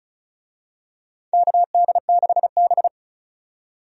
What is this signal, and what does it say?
Morse code, heard as a keyed tone, about 36 wpm, 700 Hz. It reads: KD6B
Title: USA Call Signs - 4 Characters - 35wpm